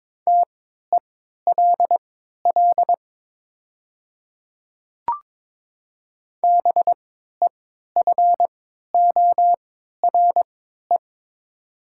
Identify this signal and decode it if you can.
Morse code, heard as a keyed tone, about 22 wpm, spaced as for 12 wpm, 700 Hz. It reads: TELL BEFORE
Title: 3x Character Spacing - Top 300 Words - 22wpm